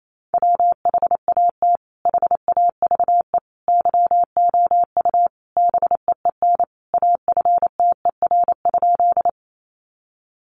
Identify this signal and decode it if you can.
Morse code, heard as a keyed tone, about 28 wpm, 700 Hz. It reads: WHAT HAVE YOU BEEN AFTER?
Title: Sentences from Top 200 Words 28wpm